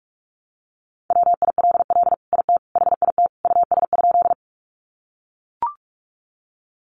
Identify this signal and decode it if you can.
Morse code, heard as a keyed tone, about 44 wpm, 700 Hz. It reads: WILL IT HIT US?
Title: Sentences from Top 1200 Words 45wpm